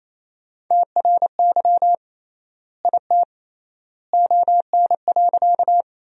Morse code, heard as a keyed tone, about 28 wpm, 700 Hz. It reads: TRY IT ON.
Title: 3x Word-Spacing - Sentences from Top 200 Words - 28wpm